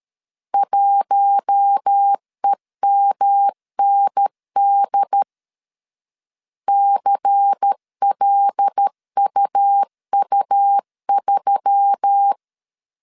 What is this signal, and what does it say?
Morse code, heard as a keyed tone, about 13 wpm, 800 Hz. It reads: 1EMND CLUU3
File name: CW / DET / 1 6 1500C S A 12wpm 3440